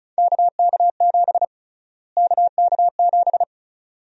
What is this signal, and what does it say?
Morse code, heard as a keyed tone, about 34 wpm, 700 Hz. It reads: KK7 KK7
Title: USA Call Sign Prefixes - Repeated 2x - 35wpm